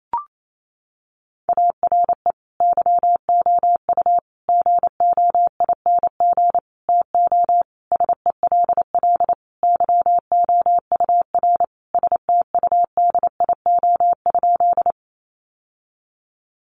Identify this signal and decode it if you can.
Morse code, heard as a keyed tone, about 28 wpm, 700 Hz. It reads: ARE YOU GOING TO SELL YOUR STUDIO?
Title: Sentences from Top 1200 Words 28wpm